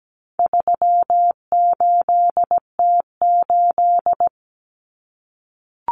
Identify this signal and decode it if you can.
Morse code, heard as a keyed tone, about 17 wpm, 700 Hz. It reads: 38T8
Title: Common Cut Numbers - 4-Digit 17wpm